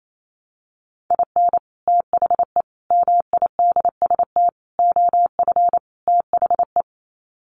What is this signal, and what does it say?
Morse code, heard as a keyed tone, about 28 wpm, 700 Hz. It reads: IN THE MIDST OF THE
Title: Sets of 5 Words 28wpm